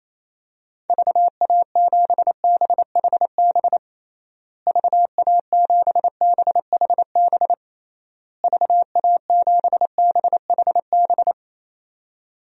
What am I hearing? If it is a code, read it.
Morse code, heard as a keyed tone, about 27 wpm, 700 Hz. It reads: VA7BHB VA7BHB VA7BHB
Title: DXCC-100 Call Signs - 6 Characters - Repeated 3x - 28wpm